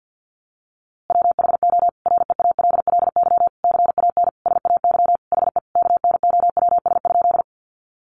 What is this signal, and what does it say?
Morse code, heard as a keyed tone, about 50 wpm, 700 Hz. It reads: WHO REALLY CAN SAY HE KNOWS?
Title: A Sound Of Thunder 50wpm